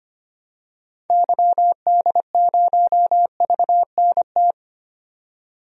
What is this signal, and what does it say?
Morse code, heard as a keyed tone, about 25 wpm, 700 Hz. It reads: YD0VNT
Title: DXCC-100 Call Signs - 6 Characters - 25wpm